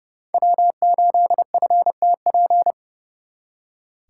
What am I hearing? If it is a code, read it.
Morse code, heard as a keyed tone, about 38 wpm, 700 Hz. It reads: W8FTP